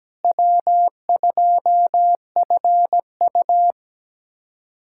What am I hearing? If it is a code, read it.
Morse code, heard as a keyed tone, about 17 wpm, 700 Hz. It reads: W2FU
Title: USA Callsign DE Callsign - 4 Characters - 17wpm